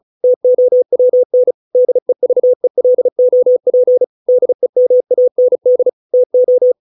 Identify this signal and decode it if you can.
Morse code, heard as a keyed tone, about 34 wpm, 500 Hz. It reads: TOWN DEVELOP DEMAND TO